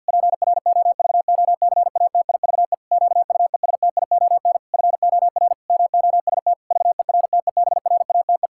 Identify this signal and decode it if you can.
Morse code, heard as a keyed tone, about 51 wpm, 700 Hz. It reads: PROVOCATIVE QUESTION FOR MOST VERTEBRATE
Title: Pangram Challenge v2 - 50wpm